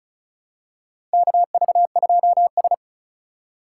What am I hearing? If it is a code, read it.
Morse code, heard as a keyed tone, about 36 wpm, 700 Hz. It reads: KV2S
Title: USA Call Signs - 2x Character Spacing - 35wpm